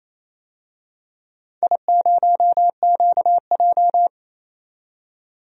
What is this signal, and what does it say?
Morse code, heard as a keyed tone, about 28 wpm, 700 Hz. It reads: I0QJ
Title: DXCC-100 Call Signs - 4 Characters - 28wpm